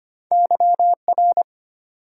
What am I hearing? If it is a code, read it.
Morse code, heard as a keyed tone, about 25 wpm, 700 Hz. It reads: YR
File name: Two Letter-Number Combinations - 25wpm